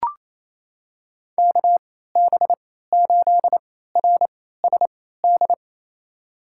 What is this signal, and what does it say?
Morse code, heard as a keyed tone, about 28 wpm, spaced as for 16 wpm, 700 Hz. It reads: KB8RSD
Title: USA Call Signs - 6 Characters - 3x Character Spacing - 28wpm